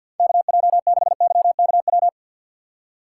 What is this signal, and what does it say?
Morse code, heard as a keyed tone, about 50 wpm, 700 Hz. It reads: KJ6YKW
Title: USA Call Signs - 6 Characters - 3x Character Spacing - 50wpm